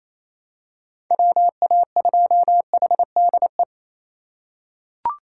Morse code, heard as a keyed tone, about 28 wpm, 690 Hz. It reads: WA2HDE
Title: USA Call Signs - 6 Characters - 4x Character Spacing - 28wpm